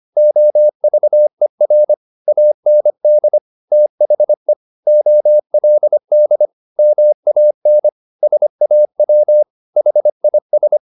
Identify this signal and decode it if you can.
Morse code, heard as a keyed tone, about 25 wpm, 600 Hz. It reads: OVER AND THE OLD MAN SAW HIS